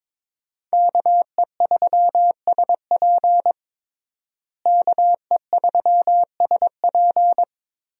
Morse code, heard as a keyed tone, about 22 wpm, 700 Hz. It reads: KE3SP KE3SP
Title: USA Call Signs - 5 Characters - Repeated 3x - 22wpm